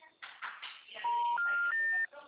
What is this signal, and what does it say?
Special information tone, with noise.